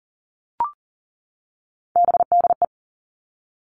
Morse code, heard as a keyed tone, about 41 wpm, 700 Hz. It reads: BDE
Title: Three Challenging Letter-Number Combinations 40wpm